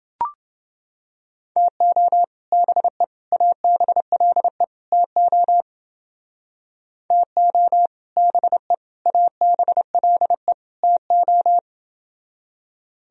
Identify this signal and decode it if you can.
Morse code, heard as a keyed tone, about 29 wpm, 700 Hz. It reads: TO BE ABLE TO TO BE ABLE TO